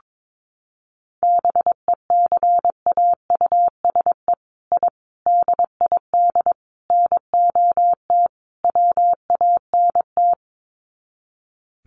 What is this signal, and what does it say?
Morse code, heard as a keyed tone, about 22 wpm, 700 Hz. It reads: BECAUSE I DID NOT WANT
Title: Sets of 5 Words Encore2 22wpm